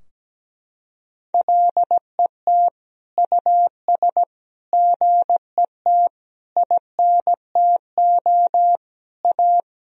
Morse code, heard as a keyed tone, about 17 wpm, 700 Hz. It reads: LET US GET INTO A
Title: Sentences from Top 1000 Words - Review Part 4 - 17wpm